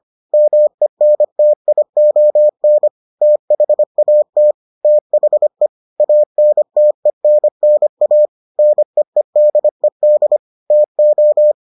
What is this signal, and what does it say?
Morse code, heard as a keyed tone, about 25 wpm, 600 Hz. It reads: MENTION THAT THE ANTENNA NEEDED TO